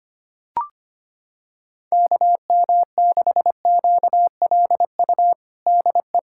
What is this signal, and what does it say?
Morse code, heard as a keyed tone, about 25 wpm, 700 Hz. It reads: KM6QLU DE
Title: USA Callsign DE Callsign 25wpm